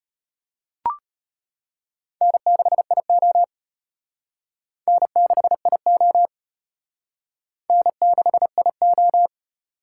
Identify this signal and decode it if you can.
Morse code, heard as a keyed tone, about 31 wpm, 700 Hz. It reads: N6IO N6IO N6IO
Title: Speed Racing - USA Call Signs 25wpm